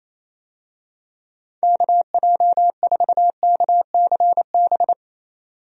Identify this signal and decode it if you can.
Morse code, heard as a keyed tone, about 28 wpm, 700 Hz. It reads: KJ4KCB
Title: USA Call Signs - 6 Characters - 28wpm